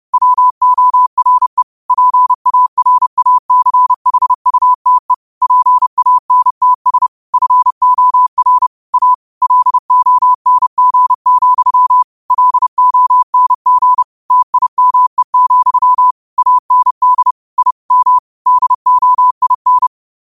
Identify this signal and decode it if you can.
Morse code, heard as a keyed tone, about 30 wpm, 1000 Hz. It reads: WORE PARACHUTE PANTS FOR A LONG, LONG TIME, AND I M DOIN